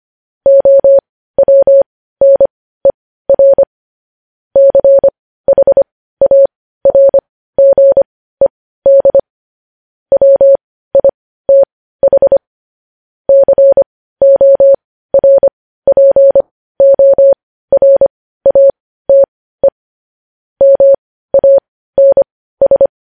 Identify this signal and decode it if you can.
Morse code, heard as a keyed tone, about 25 wpm, spaced as for 15 wpm, 550 Hz. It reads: OWNER CHARGED WITH CORPORATE MANS
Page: news headlines in morse code at 15 wpm